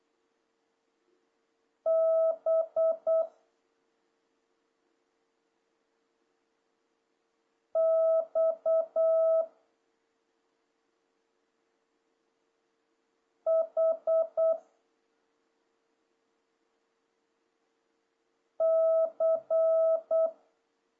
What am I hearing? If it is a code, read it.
Morse code, heard as a keyed tone, about 8 wpm, 1300 Hz. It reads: B X H C
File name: jh8cbh / 2026cw